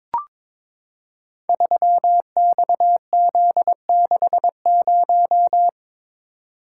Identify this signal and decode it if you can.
Morse code, heard as a keyed tone, about 22 wpm, 700 Hz. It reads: 3XZ60
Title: Code Groups - 5 Letter:Numbers - 22wpm